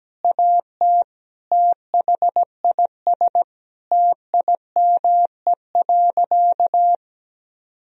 Morse code, heard as a keyed tone, about 17 wpm, 700 Hz. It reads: AT THIS TIME.
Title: Sentences from Top 1000 Words - Review Part 3 - 17wpm